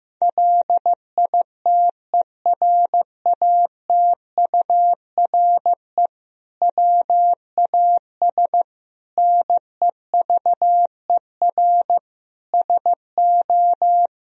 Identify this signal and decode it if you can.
Morse code, heard as a keyed tone, about 15 wpm, 700 Hz. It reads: LITERATURE WAS NEVER SO
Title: Sentences from Top 1000 Words - Review Part 3 - 15wpm